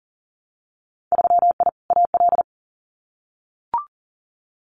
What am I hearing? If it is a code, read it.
Morse code, heard as a keyed tone, about 37 wpm, 700 Hz. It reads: 3I AL
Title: Winter Field Day Exchanges 40wpm